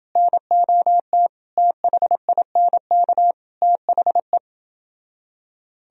Morse code, heard as a keyed tone, about 29 wpm, 700 Hz. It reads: NOT THINK THE